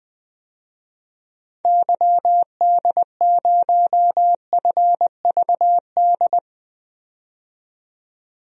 Morse code, heard as a keyed tone, about 21 wpm, 700 Hz. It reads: YD0FVD
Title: DXCC-100 Call Signs - 6 Characters - 20wpm